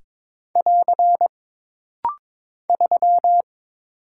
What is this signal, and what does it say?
Morse code, heard as a keyed tone, about 22 wpm, 700 Hz. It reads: <AR> 3
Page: This rapid-fire timing is geared to push your ICR (Instant Character Recognition) skills.